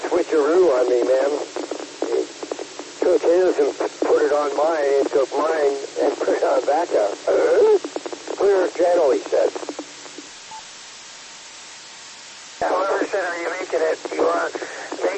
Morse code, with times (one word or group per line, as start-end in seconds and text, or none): none